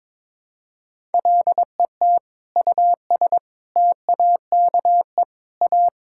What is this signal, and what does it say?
Morse code, heard as a keyed tone, about 22 wpm, 700 Hz. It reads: LET US TAKE A
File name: Sentences with 7-Word Phrases - 22wpm